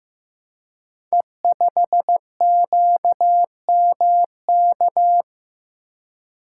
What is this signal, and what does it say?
Morse code, heard as a keyed tone, about 15 wpm, 700 Hz. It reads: E5QMK
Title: Code Groups - 5 Letter:Numbers - 15wpm